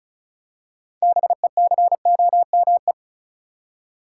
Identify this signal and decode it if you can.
Morse code, heard as a keyed tone, about 36 wpm, 700 Hz. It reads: BECOME
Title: Six Letter Words - 35wpm